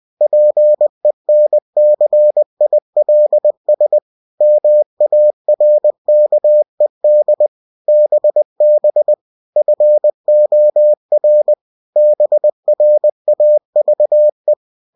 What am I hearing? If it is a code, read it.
Morse code, heard as a keyed tone, about 20 wpm, 600 Hz. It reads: PENCILS MARKED BB FOR BRAVE
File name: cw-books / winnie-the-pooh / winnie-20wpm-page-0109